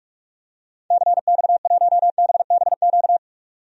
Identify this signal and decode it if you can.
Morse code, heard as a keyed tone, about 43 wpm, 700 Hz. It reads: CX1BBQ
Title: DXCC-100 Call Signs - 6 Characters - 45wpm